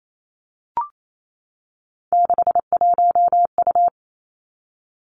Morse code, heard as a keyed tone, about 28 wpm, 700 Hz. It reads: 61U
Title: Three Challenging Letter-Number Combinations 28wpm